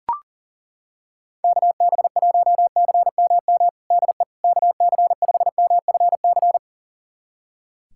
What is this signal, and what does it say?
Morse code, heard as a keyed tone, about 40 wpm, 700 Hz. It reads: KB1CMM DE KC5MFC
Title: USA Callsign DE Callsign - 6 Characters - 40wpm